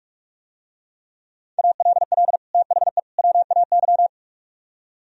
Morse code, heard as a keyed tone, about 44 wpm, 700 Hz. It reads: ALL THE WAY